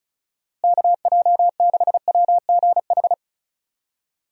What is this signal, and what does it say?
Morse code, heard as a keyed tone, about 36 wpm, 700 Hz. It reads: KJ6WGH
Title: USA Call Signs - 6 Characters - Repeated 2x - 35wpm